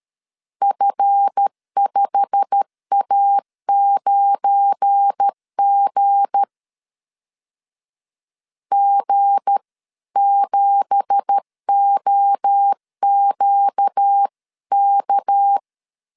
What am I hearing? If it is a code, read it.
Morse code, heard as a keyed tone, about 13 wpm, 800 Hz. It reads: F5A9G G7OQK
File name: CW / DET / 1 9 1500C S C 12wpm 3811